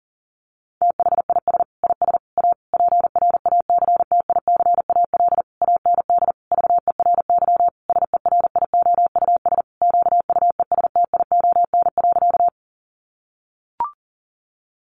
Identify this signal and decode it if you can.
Morse code, heard as a keyed tone, about 40 wpm, 700 Hz. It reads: THIS IS A PRACTICAL AND VERY SERIOUS QUESTION.
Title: Sentences from Top 1400 Words 40wpm